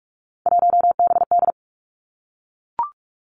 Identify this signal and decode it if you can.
Morse code, heard as a keyed tone, about 46 wpm, 700 Hz. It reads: 1BD
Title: Three Challenging Letter-Number Combinations 45wpm